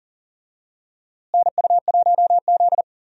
Morse code, heard as a keyed tone, about 40 wpm, 700 Hz. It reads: NU1Z
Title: USA Call Signs - 4 Characters - Repeated 2x - 40wpm